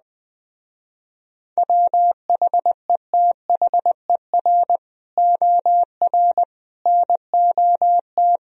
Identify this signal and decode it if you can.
Morse code, heard as a keyed tone, about 20 wpm, 700 Hz. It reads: WHETHER OR NOT
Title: Speed-Racing - Sets of 3 Words 17wpm